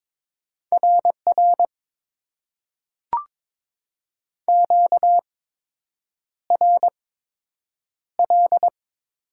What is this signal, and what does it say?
Morse code, heard as a keyed tone, about 21 wpm, spaced as for 8 wpm, 700 Hz. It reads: RR QRL
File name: Top 90 QSO Elements - 8x Character Spacing - 22wpm